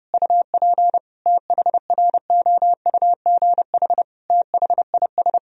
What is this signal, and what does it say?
Morse code, heard as a keyed tone, about 30 wpm, 700 Hz. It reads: UP THROUGH THIS
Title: Sentences from Top 1000 Words - Review Part 2 - 30wpm